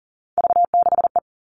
Morse code, heard as a keyed tone, about 40 wpm, 700 Hz. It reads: V6E
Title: All Cut Numbers - 3-Digit 40wpm